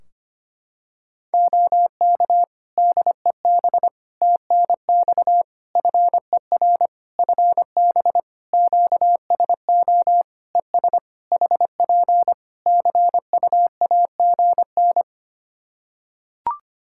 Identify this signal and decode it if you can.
Morse code, heard as a keyed tone, about 25 wpm, 700 Hz. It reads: OK DEB TNX FER FB QSO ES HP CU AGN
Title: Practice QSO - Exchange 4 - Ending - 25wpm